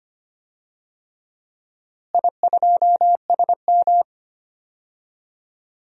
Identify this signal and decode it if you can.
Morse code, heard as a keyed tone, about 25 wpm, 700 Hz. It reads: I2SM
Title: DXCC-100 Call Signs - 4 Characters - Repeated 3x - 25wpm